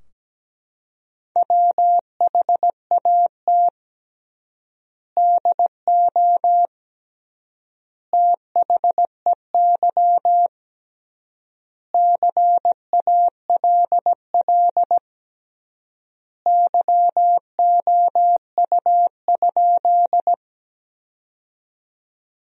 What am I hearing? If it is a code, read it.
Morse code, heard as a keyed tone, about 17 wpm, 700 Hz. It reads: WHAT DO THEY CALL YOU?
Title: 3x Word-Spacing - Sentences from Top 100 Words - 17wpm